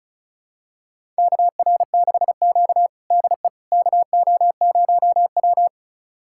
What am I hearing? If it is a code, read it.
Morse code, heard as a keyed tone, about 36 wpm, 700 Hz. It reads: KR6Q DE KO0W
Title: USA Callsign DE Callsign - 4 Characters - 35wpm